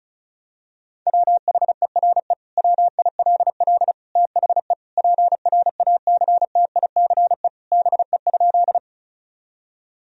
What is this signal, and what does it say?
Morse code, heard as a keyed tone, about 36 wpm, 700 Hz. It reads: WHERE WILL THE PRACTICE BE?
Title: Sentences from Top 1000 Words - Review Part 4 - 35wpm